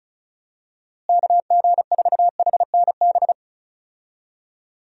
Morse code, heard as a keyed tone, about 34 wpm, 700 Hz. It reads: KG4HNB
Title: USA Call Signs - 6 Characters - Repeated 3x - 35wpm